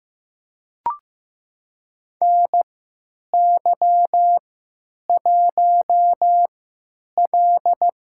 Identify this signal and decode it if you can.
Morse code, heard as a keyed tone, about 15 wpm, spaced as for 9 wpm, 700 Hz. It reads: NY1L
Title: USA Call Signs - 4 Characters - 3x Character Spacing - 15wpm